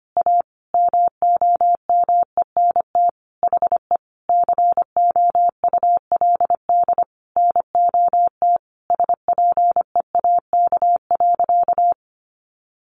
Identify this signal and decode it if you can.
Morse code, heard as a keyed tone, about 25 wpm, 700 Hz. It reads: A MOMENT HE COULD NOT SPEAK.
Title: Sentences from Top 1100 Words 25wpm